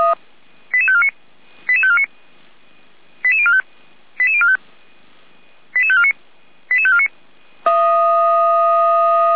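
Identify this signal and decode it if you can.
Five-tone selective call (ZVEI), heard as quick runs of five tones, tones of about 70 ms. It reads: 88549 88549 88045 88045 88549 88549
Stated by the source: Received on an SDR; ZVEI five-tone call 88415.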